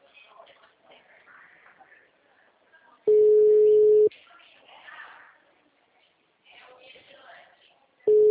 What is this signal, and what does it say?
Ringback tone (Europe, 425 Hz).